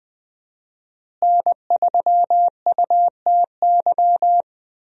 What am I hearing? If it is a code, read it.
Morse code, heard as a keyed tone, about 20 wpm, 700 Hz. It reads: N3UTY
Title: USA Call Signs - 5 Characters - Repeated 2x - 20wpm